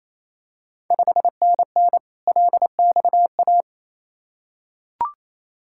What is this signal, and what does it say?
Morse code, heard as a keyed tone, about 28 wpm, 700 Hz. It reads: 5NN LXA
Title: Ontario QSO Party Exchange 28wpm